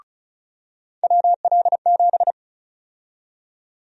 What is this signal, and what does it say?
Morse code, heard as a keyed tone, about 34 wpm, 700 Hz. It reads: WL7
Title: USA Call Sign Prefixes - 35wpm